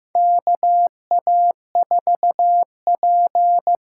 Morse code, heard as a keyed tone, about 15 wpm, 700 Hz. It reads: KA4P
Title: USA Call Signs - 6 Characters - 6x Character Spacing - 15wpm